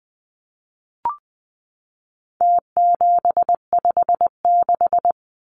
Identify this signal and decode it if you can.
Morse code, heard as a keyed tone, about 20 wpm, 700 Hz. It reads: T756
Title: Numbers - Contest Style - 4-Digit 20wpm